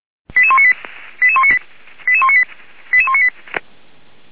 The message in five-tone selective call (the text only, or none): 88198 88198 88198 88198